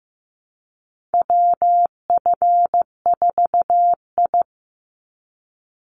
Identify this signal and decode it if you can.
Morse code, heard as a keyed tone, about 15 wpm, 700 Hz. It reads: WF4I
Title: NA SOTA Chaser Callsigns 15wpm